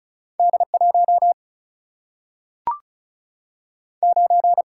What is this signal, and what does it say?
Morse code, heard as a keyed tone, about 36 wpm, 700 Hz. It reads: D1 9
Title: Single-Letter Single-Number 35wpm